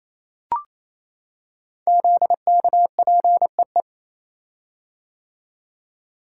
Morse code, heard as a keyed tone, about 27 wpm, 700 Hz. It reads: ZKPEE
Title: Code Groups - 5 Letters - 28wpm